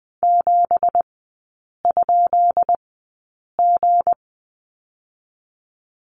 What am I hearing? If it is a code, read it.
Morse code, heard as a keyed tone, about 20 wpm, spaced as for 8 wpm, 700 Hz. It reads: 7?G
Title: 212 Lesson 34e - Number 8 - Three Characters 20wpm